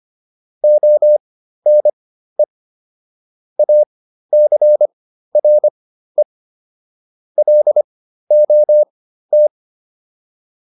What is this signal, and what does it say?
Morse code, heard as a keyed tone, about 25 wpm, spaced as for 13 wpm, 600 Hz. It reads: ONE ACRE LOT